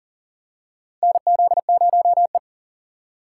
Morse code, heard as a keyed tone, about 40 wpm, 700 Hz. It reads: NZ0E
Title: USA Call Signs - 4 Characters - 4x Character Spacing - 40wpm